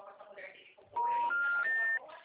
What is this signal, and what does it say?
Special information tone, with speech.
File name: out- -2002- - - .133